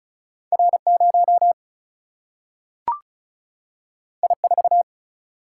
Morse code, heard as a keyed tone, about 36 wpm, 700 Hz. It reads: R0 I4
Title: Single-Letter Single-Number 35wpm